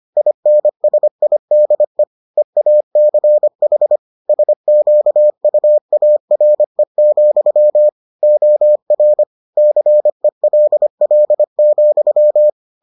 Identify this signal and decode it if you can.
Morse code, heard as a keyed tone, about 25 wpm, 600 Hz. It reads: INSIDE EACH SQUARE, OR CELL,